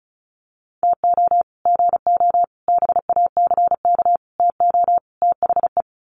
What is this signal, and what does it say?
Morse code, heard as a keyed tone, about 34 wpm, 700 Hz. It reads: TO GO BACK TO THE
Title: Sets of 5 Words 35wpm